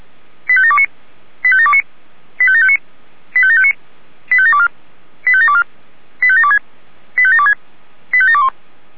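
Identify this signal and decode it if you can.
Five-tone selective call (ZVEI), heard as quick runs of five tones, tones of about 70 ms. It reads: 85729 85729 85759 85759 85724 85724 85726 85726 85721
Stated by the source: ZVEI five-tone call received by SDR.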